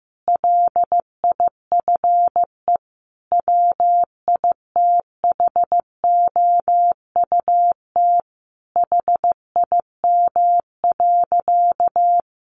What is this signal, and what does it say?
Morse code, heard as a keyed tone, about 15 wpm, 700 Hz. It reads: LIFE WITHOUT HIM.
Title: Sentences from Top 1400 Words 15wpm